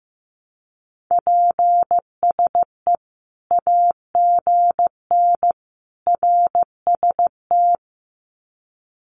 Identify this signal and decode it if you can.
Morse code, heard as a keyed tone, about 15 wpm, 700 Hz. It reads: PSE AGN RST
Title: Common QSO Phrases 15wpm